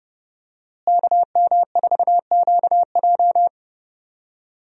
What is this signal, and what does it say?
Morse code, heard as a keyed tone, about 30 wpm, 700 Hz. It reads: KM4QJ
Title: USA Call Signs - 5 Characters - Repeated 2x - 30wpm